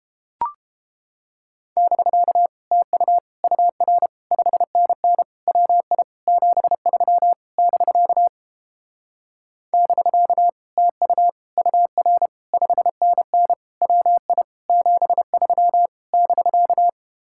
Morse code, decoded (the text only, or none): <BK> TU UR 5NN WI 73 <BK> <BK> TU UR 5NN WI 73 <BK>